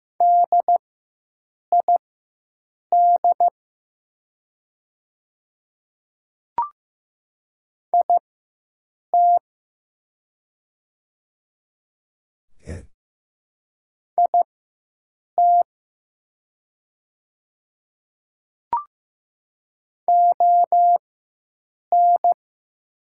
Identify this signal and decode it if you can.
Morse code, heard as a keyed tone, about 15 wpm, spaced as for 7 wpm, 700 Hz. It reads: DID IT IT ON
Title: 4x Character Spacing - Top 200 Words - 15wpm